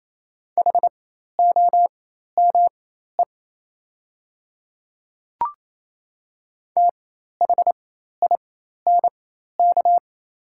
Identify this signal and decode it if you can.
Morse code, heard as a keyed tone, about 28 wpm, spaced as for 13 wpm, 700 Hz. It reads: HOME THINK
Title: 4x Character Spacing - Top 200 Words - 28wpm